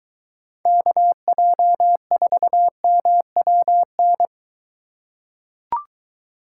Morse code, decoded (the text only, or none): KJ4MWN